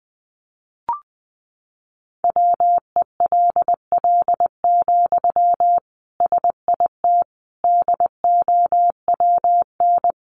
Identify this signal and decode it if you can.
Morse code, heard as a keyed tone, about 19 wpm, 700 Hz. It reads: WELL, SIT DOWN
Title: Sentences from Top 500 Words 20wpm